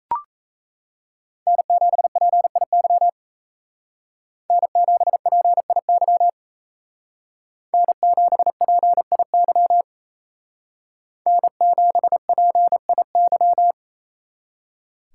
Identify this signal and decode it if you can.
Morse code, heard as a keyed tone, about 34 wpm, 700 Hz. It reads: N7PIY N7PIY N7PIY N7PIY
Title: Speed Racing - USA 1x3 Call Signs 28wpm